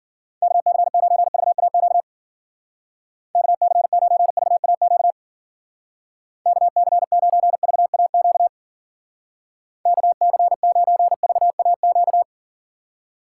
Morse code, heard as a keyed tone, about 50 wpm, 700 Hz. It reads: KC9VAQ KC9VAQ KC9VAQ KC9VAQ
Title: Speed Racing - USA 2x3 Call Signs 40wpm